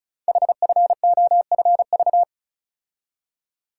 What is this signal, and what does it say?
Morse code, heard as a keyed tone, about 36 wpm, 700 Hz. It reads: HFOFV
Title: Code Groups - 5 Letters - 35wpm